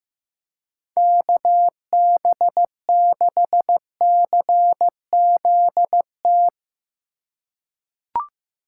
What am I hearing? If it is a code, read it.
Morse code, heard as a keyed tone, about 15 wpm, 700 Hz. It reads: KB6CZT
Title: USA Call Signs - 6 Characters - 6x Character Spacing - 15wpm